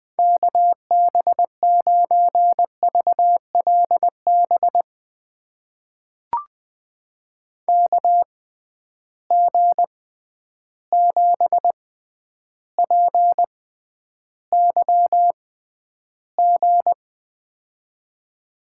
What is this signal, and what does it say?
Morse code, heard as a keyed tone, about 20 wpm, spaced as for 10 wpm, 700 Hz. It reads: KB9VLB KG7PYG
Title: USA Call Signs - 6 Characters - 6x Character Spacing - 20wpm